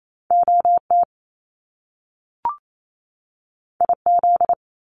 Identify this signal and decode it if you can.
Morse code, heard as a keyed tone, about 28 wpm, 700 Hz. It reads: O T I Z
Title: Top Two Letter Combinations in Words 28wpm